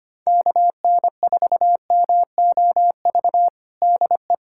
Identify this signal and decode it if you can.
Morse code, heard as a keyed tone, about 25 wpm, 700 Hz. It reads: KN4MOV DE
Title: USA Callsign DE Callsign - 6 Characters - 25wpm